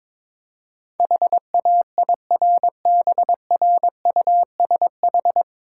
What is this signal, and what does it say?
Morse code, heard as a keyed tone, about 24 wpm, 700 Hz. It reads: HAIRBRUSH
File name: Easy Words - Speed-Racing with Hints - 22wpm